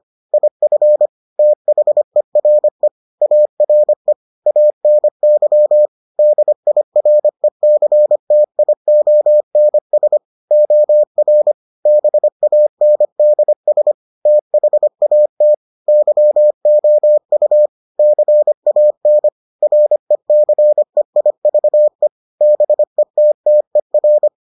Morse code, heard as a keyed tone, about 25 wpm, 600 Hz. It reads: IF THERE ARE ANY DIRECTIONS OR BANDS THAT YOU CAN RECEIVE BETTER